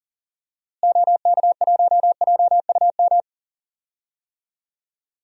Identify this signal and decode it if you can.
Morse code, heard as a keyed tone, about 40 wpm, 700 Hz. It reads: OK1JUM
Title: DXCC-100 Call Signs - 6 Characters - Repeated 2x - 40wpm